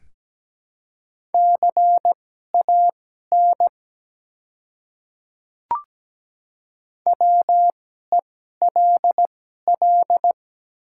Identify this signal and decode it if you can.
Morse code, heard as a keyed tone, about 17 wpm, spaced as for 12 wpm, 700 Hz. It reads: CAN WELL
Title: 2x Character Spacing - Top 200 Words - 17wpm